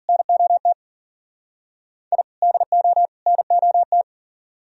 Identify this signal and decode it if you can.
Morse code, heard as a keyed tone, about 51 wpm, 700 Hz. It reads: NOT I DO NOT